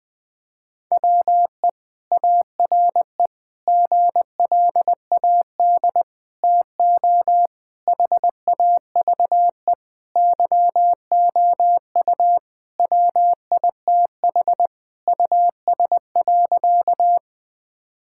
Morse code, heard as a keyed tone, about 20 wpm, 700 Hz. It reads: WE ARE GLAD TO HAVE YOU WITH US.
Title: Sentences with 5-Word Phrases - 20wpm